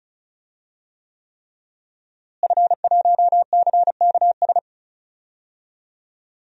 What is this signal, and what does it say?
Morse code, heard as a keyed tone, about 34 wpm, 700 Hz. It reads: F1CKS
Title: DXCC-100 Call Signs - 5 Characters - Repeated 3x - 35wpm